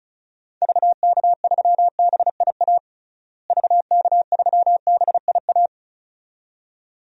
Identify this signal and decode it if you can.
Morse code, heard as a keyed tone, about 34 wpm, 700 Hz. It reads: VK3BIA VK3BIA
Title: DXCC-100 Call Signs - 6 Characters - Repeated 2x - 35wpm